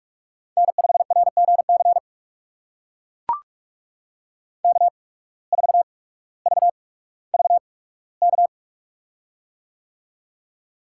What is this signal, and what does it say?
Morse code, heard as a keyed tone, about 42 wpm, spaced as for 19 wpm, 700 Hz. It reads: N5RGC K4VVK
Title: USA Call Signs - 5 Characters - 8x Character Spacing - 45wpm